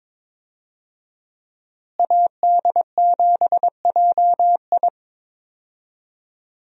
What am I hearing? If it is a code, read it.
Morse code, heard as a keyed tone, about 22 wpm, 700 Hz. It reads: AD7JI
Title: USA Call Signs - 5 Characters - Repeated 2x - 22wpm